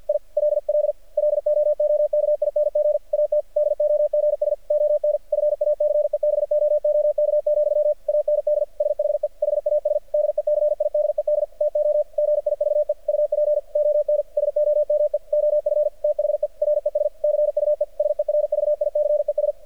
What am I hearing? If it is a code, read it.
Morse code, heard as a keyed tone, about 53 wpm, 600 Hz. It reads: I LL LOOKING AT DOGS ON FACEBOOK, AND SHE HAS DECIDED TO GIVE UP ON SOME OF THE RESCUE SERVICES
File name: archived / CRSnet